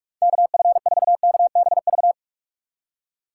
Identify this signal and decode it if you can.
Morse code, heard as a keyed tone, about 44 wpm, 700 Hz. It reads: KF4KBV
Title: USA Call Signs - 6 Characters - 4x Character Spacing - 45wpm